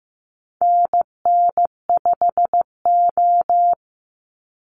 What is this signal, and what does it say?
Morse code, heard as a keyed tone, about 15 wpm, 700 Hz. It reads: NN5O
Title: Call Signs Repeated 2x - 15wpm